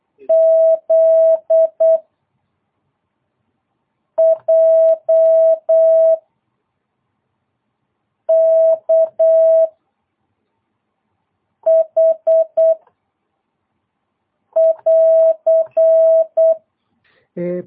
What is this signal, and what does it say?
Morse code, heard as a keyed tone, about 8 wpm, 650 Hz. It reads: Z J K H <AR>
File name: jh8cbh / 2022cw